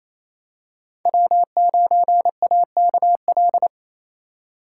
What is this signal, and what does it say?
Morse code, heard as a keyed tone, about 28 wpm, 700 Hz. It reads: W9AKL
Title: USA Call Signs - 5 Characters - 2x Character Spacing - 28wpm